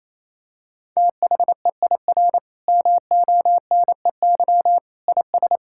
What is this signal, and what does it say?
Morse code, heard as a keyed tone, about 28 wpm, 700 Hz. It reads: THEIR MONEY IS
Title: Sentences from Top 1600 Words 28wpm